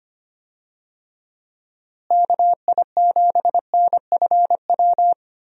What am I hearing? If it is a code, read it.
Morse code, heard as a keyed tone, about 25 wpm, 700 Hz. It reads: KI7NFW
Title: USA Call Signs - 6 Characters - Repeated 2x - 25wpm